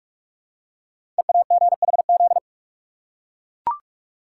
Morse code, heard as a keyed tone, about 44 wpm, 700 Hz. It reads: EAGHZ
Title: Code Groups - 5 Letters - 45wpm